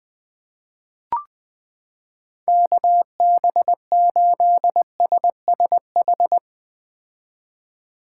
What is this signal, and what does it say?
Morse code, heard as a keyed tone, about 20 wpm, 700 Hz. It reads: KB8SSH
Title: 2023 POTA Hunter Call Signs - 20wpm